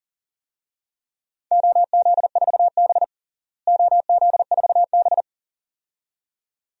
Morse code, heard as a keyed tone, about 40 wpm, 700 Hz. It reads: OZ4B OZ4B
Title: DXCC-100 Call Signs - 4 Characters - Repeated 2x - 40wpm